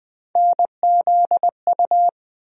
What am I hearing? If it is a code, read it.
Morse code, heard as a keyed tone, about 20 wpm, 700 Hz. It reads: NZU
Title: USA Call Sign Postfixes - 3 Letters 20wpm